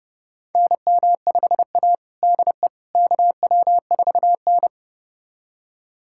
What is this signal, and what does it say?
Morse code, heard as a keyed tone, about 30 wpm, 700 Hz. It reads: NM5A DE KW4N
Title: USA Callsign DE Callsign - 4 Characters - 30wpm